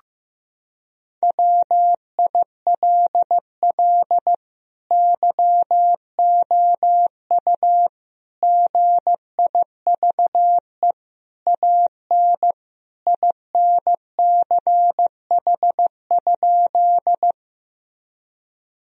Morse code, heard as a keyed tone, about 15 wpm, 700 Hz. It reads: WILL YOU GIVE AN INCH?
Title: Sentences from Top 1000 Words - Review Part 3 - 15wpm